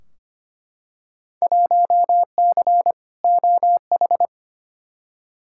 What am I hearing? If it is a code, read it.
Morse code, heard as a keyed tone, about 25 wpm, 700 Hz. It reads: 1C OH
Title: Field Day Exchanges - 25wpm